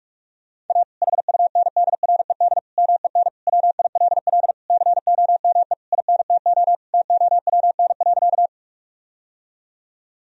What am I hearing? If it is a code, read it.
Morse code, heard as a keyed tone, about 47 wpm, 700 Hz. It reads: A HUNDRED MEN WILL COME INTO TOWN.
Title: Sentences from Top 1000 Words - Review Part 2 - 45wpm